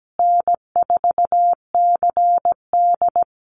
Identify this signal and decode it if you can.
Morse code, heard as a keyed tone, about 17 wpm, 700 Hz. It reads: N4CD
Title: Call Signs Repeated 3x - 17wpm